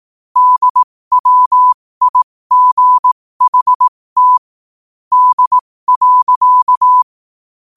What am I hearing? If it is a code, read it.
Morse code, heard as a keyed tone, about 18 wpm, spaced as for 15 wpm, 1000 Hz. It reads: DWIGHT D.
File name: qotdfiles / 15wpm